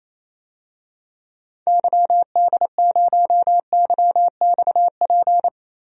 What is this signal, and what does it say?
Morse code, heard as a keyed tone, about 28 wpm, 700 Hz. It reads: YD0YXP
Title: DXCC-100 Call Signs - 6 Characters - Repeated 2x - 28wpm